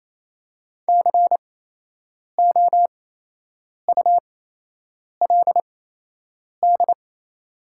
Morse code, heard as a keyed tone, about 28 wpm, spaced as for 8 wpm, 700 Hz. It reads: COULD
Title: 8x Character Spacing - Top 300 Words - 28wpm